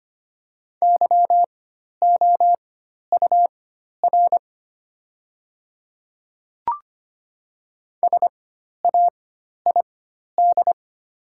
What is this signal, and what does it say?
Morse code, heard as a keyed tone, about 25 wpm, spaced as for 12 wpm, 700 Hz. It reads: YOUR SAID
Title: 4x Character Spacing - Top 100 Words - 25wpm